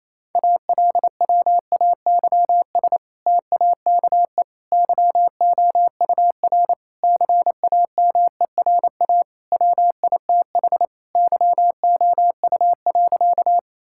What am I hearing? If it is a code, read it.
Morse code, heard as a keyed tone, about 28 wpm, 700 Hz. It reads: ALWAYS TAKE YOUR CAMERA WITH YOU.
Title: Sentences from Top 1000 Words - Review Part 5 - 28wpm